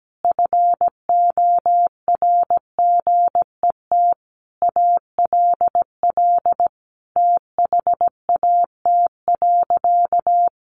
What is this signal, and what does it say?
Morse code, heard as a keyed tone, about 17 wpm, 700 Hz. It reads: FORGET ALL THAT.
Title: Sentences from Top 1200 Words 17wpm